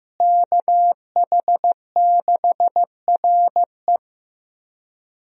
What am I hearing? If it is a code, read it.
Morse code, heard as a keyed tone, about 15 wpm, 700 Hz. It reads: KH6RE
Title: USA Call Signs - 5 Characters - Repeated 2x - 15wpm